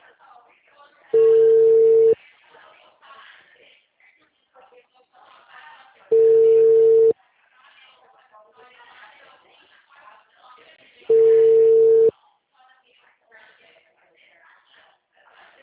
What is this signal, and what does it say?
Ringback tone (China, 450 Hz), with speech.
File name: out- -2015- - - .51553